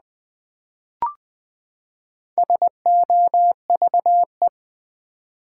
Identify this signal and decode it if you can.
Morse code, heard as a keyed tone, about 20 wpm, 700 Hz. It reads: SOVE
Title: Build A Word - 4 Letters At A Time - 11-Letter Words - 20wpm